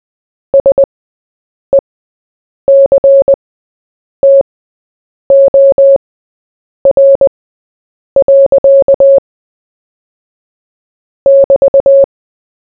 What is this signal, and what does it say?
Morse code, heard as a keyed tone, about 20 wpm, spaced as for 8 wpm, 550 Hz. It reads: SECTOR. <BT>
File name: Short-Story-4-8wpm